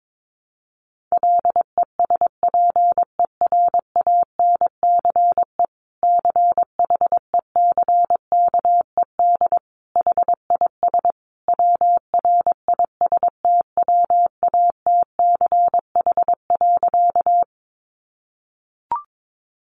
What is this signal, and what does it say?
Morse code, heard as a keyed tone, about 22 wpm, 700 Hz. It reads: LESPERANCE CHECKED HIS WRISTWATCH.
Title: A Sound Of Thunder 22wpm